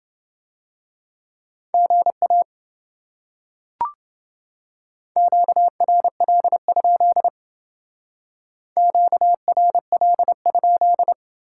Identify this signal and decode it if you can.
Morse code, heard as a keyed tone, about 29 wpm, 700 Hz. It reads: GA QRL? QRL?